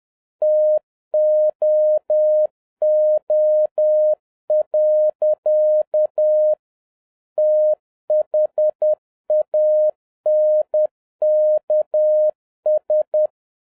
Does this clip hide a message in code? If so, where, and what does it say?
Yes: Morse code from 0.4 to 13.3 s, heard as a keyed tone, about 10 wpm, 610 Hz. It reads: TOO. THANKS